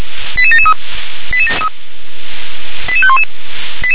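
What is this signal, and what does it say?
Five-tone selective call (ZVEI), heard as quick runs of five tones, tones of about 70 ms. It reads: 88773 88511
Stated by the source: Received on SDR as a ZVEI five-tone selective call, address 88743.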